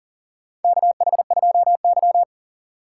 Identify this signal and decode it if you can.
Morse code, heard as a keyed tone, about 40 wpm, 700 Hz. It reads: KH2Y
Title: USA Call Signs - 4 Characters - 2x Character Spacing - 40wpm